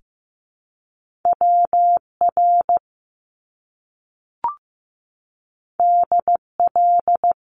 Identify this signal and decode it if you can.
Morse code, heard as a keyed tone, about 15 wpm, 700 Hz. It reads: WR DL
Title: Top Two Letter Combinations in Words 15wpm